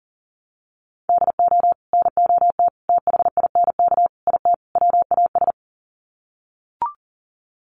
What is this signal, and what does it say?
Morse code, heard as a keyed tone, about 40 wpm, 700 Hz. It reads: DO NOT THINK IT WAS
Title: Sets of 5 Words Encore 40wpm